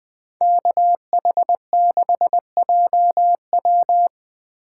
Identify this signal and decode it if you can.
Morse code, heard as a keyed tone, about 20 wpm, 700 Hz. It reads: KH6JW
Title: Speed Racing - USA 2x2 Call Signs 17wpm